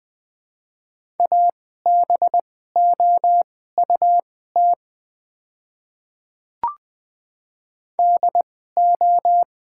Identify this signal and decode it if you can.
Morse code, heard as a keyed tone, about 20 wpm, spaced as for 15 wpm, 700 Hz. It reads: ABOUT DO
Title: 2x Character Spacing - Top 100 Words - 20wpm